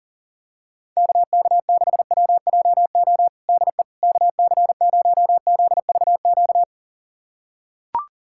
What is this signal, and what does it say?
Morse code, heard as a keyed tone, about 40 wpm, 700 Hz. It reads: KK6WJO DE KC0ZVQ
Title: USA Callsign DE Callsign 40wpm